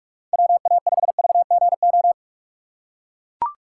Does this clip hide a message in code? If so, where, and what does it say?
Yes: Morse code from 0.3 to 2.1 s, heard as a keyed tone, about 43 wpm, 700 Hz. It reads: WA5VGO